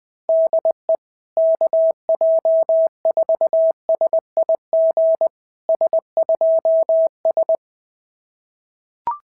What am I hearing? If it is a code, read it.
Morse code, heard as a keyed tone, about 18 wpm, 650 Hz. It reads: DE KJ4SIG S2S
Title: SOTA QSO Practice 20wpm